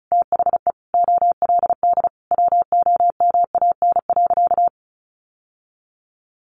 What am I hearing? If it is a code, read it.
Morse code, heard as a keyed tone, about 34 wpm, 700 Hz. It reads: THE OLD WOMAN.
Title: Sentences from Top 1100 Words 35wpm